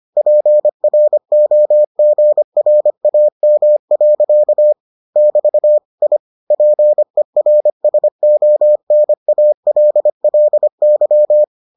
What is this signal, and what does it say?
Morse code, heard as a keyed tone, about 25 wpm, 600 Hz. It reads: PROGRAM. <BT> I PERSONALLY